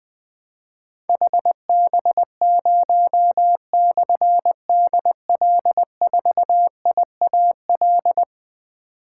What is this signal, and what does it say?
Morse code, heard as a keyed tone, about 20 wpm, 700 Hz. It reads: HB0/DL4IAL
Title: Difficult Call Signs 20wpm